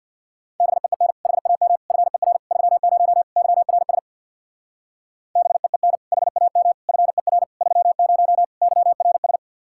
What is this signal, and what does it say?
Morse code, heard as a keyed tone, about 65 wpm, 700 Hz. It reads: BEEN HAM FER 30 YRS BEEN HAM FER 30 YRS